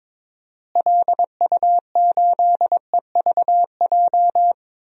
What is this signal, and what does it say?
Morse code, heard as a keyed tone, about 22 wpm, 700 Hz. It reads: LU8EVJ
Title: DXCC-100 Call Signs - 6 Characters - Repeated 2x - 22wpm